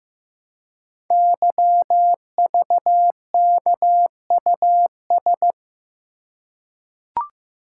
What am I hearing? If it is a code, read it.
Morse code, heard as a keyed tone, about 15 wpm, 690 Hz. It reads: YVKUS
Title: Code Groups - 5 Letters - 15wpm